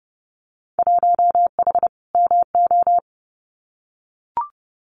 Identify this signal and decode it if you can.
Morse code, heard as a keyed tone, about 30 wpm, 700 Hz. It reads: 1H MO
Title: Winter Field Day Exchanges 30wpm